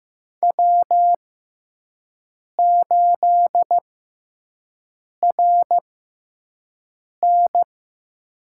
Morse code, heard as a keyed tone, about 15 wpm, spaced as for 5 wpm, 700 Hz. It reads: W8RN
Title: USA Call Signs - 6x Character Spacing - 15wpm